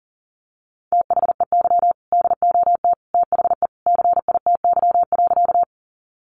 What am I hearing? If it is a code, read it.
Morse code, heard as a keyed tone, about 40 wpm, 700 Hz. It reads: THEY DOT THE CITY.
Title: Sentences from Top 1500 Words 40wpm